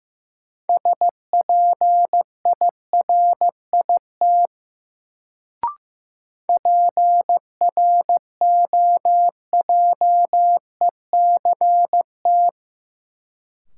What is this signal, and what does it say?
Morse code, heard as a keyed tone, about 15 wpm, 700 Hz. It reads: SPIRIT PROJECT
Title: Sentences from Top 1400 Words - New Word Drill Alt - 15wpm